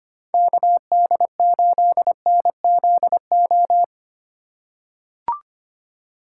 Morse code, heard as a keyed tone, about 25 wpm, 700 Hz. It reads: KD8NZO
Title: USA Call Signs - 6 Characters - 3x Character Spacing - 25wpm